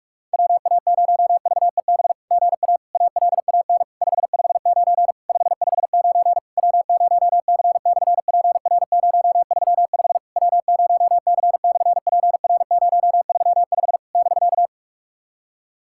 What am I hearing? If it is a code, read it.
Morse code, heard as a keyed tone, about 46 wpm, 700 Hz. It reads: WA0VEB GA ALAN 559 559 W0C/PR035 W0C/PR035 <BK>
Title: SOTA QSO Practice 45wpm